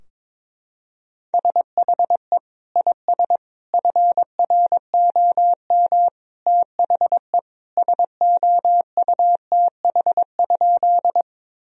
Morse code, heard as a keyed tone, about 22 wpm, 700 Hz. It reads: SHE IS FROM THE SOUTH?
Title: Sentences from Top 1000 Words - Review Part 2 - 22wpm